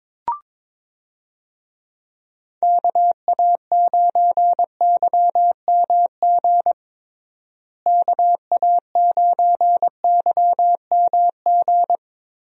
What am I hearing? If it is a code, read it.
Morse code, heard as a keyed tone, about 22 wpm, 700 Hz. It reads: KA9YMG KA9YMG
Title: USA Call Signs - 6 Characters - Repeated 3x - 22wpm